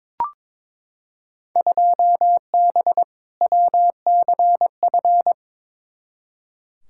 Morse code, heard as a keyed tone, about 22 wpm, 700 Hz. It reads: 2B WCF
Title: Field Day Exchanges Encore - 22wpm